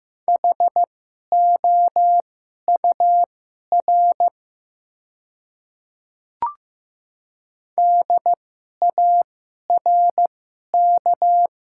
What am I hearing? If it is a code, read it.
Morse code, heard as a keyed tone, about 15 wpm, spaced as for 11 wpm, 700 Hz. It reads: HOUR DARK
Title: 2x Character Spacing - Top 500 Words - 15wpm